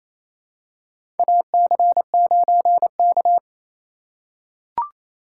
Morse code, heard as a keyed tone, about 28 wpm, 700 Hz. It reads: AC9K
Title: USA Call Signs - 4x Character Spacing - 28wpm